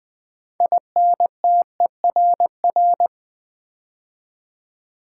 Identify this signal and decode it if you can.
Morse code, heard as a keyed tone, about 20 wpm, 700 Hz. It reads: INTERR
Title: Build A Word - 3 Letters At A Time - 9-Letter Words - 20wpm